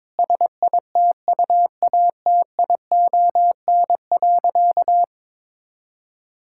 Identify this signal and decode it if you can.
Morse code, heard as a keyed tone, about 22 wpm, 700 Hz. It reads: SITUATION.
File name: practice / 22wpm / Sentences with 8-Word Phrases - 22wpm